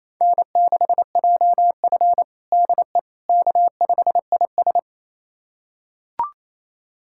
Morse code, heard as a keyed tone, about 28 wpm, 700 Hz. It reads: N6JF DE K5IS
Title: USA Callsign DE Callsign - 4 Characters - 28wpm